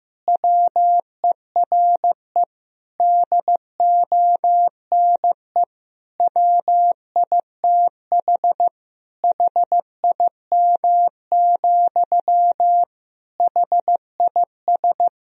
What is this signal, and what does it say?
Morse code, heard as a keyed tone, about 15 wpm, 700 Hz. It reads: WERE DONE WITH HIM, HIS
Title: Sentences from Top 1000 Words - Review Part 2 - 15wpm